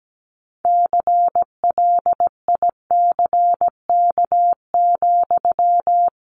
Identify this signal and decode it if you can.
Morse code, heard as a keyed tone, about 17 wpm, 700 Hz. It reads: CLICK,
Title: Sentences from Top 1100 Words 17wpm